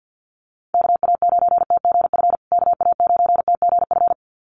Morse code, heard as a keyed tone, about 54 wpm, 700 Hz. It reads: KA9TGF KA9TGF
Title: USA Call Signs Repeated 2x 50wpm